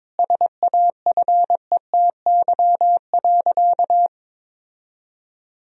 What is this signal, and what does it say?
Morse code, heard as a keyed tone, about 22 wpm, 700 Hz. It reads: SAFETY.
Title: Sentences from Top 1000 Words - Review Part 4 - 22wpm